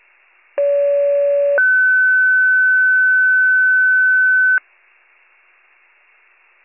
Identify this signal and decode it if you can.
Two-tone page, heard as a one-second tone, then a three-second tone, about 570 then 1500 Hz. It is heard from 0.6 to 4.6 s.